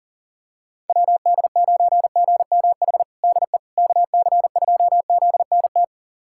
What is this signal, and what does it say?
Morse code, heard as a keyed tone, about 40 wpm, 700 Hz. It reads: WD9GMH DE KC2ZNT
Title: USA Callsign DE Callsign 40wpm